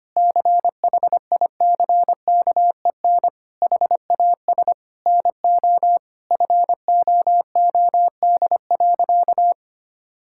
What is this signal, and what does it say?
Morse code, heard as a keyed tone, about 25 wpm, 700 Hz. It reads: CHICKEN HAS NO FOOD.
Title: Sentences from Top 1000 Words - Review Part 5 - 25wpm